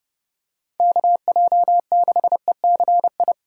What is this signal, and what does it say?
Morse code, heard as a keyed tone, about 30 wpm, 700 Hz. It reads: KJ6ECI
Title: USA Call Signs - 6 Characters - 30wpm